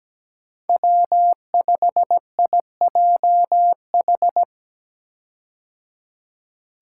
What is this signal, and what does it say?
Morse code, heard as a keyed tone, about 17 wpm, 700 Hz. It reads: W5IJH
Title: K1USN SST QSO Practice - 17wpm